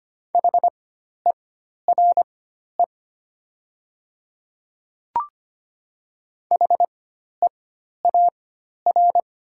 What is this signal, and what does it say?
Morse code, heard as a keyed tone, about 25 wpm, spaced as for 12 wpm, 700 Hz. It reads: HERE HEAR
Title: 4x Character Spacing - Top 300 Words - 25wpm